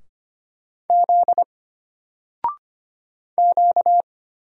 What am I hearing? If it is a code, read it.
Morse code, heard as a keyed tone, about 25 wpm, 700 Hz. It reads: Z Q
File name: Single-Letter-Number-25wpm